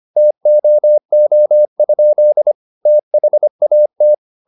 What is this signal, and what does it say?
Morse code, heard as a keyed tone, about 25 wpm, 600 Hz. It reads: TOO? THAT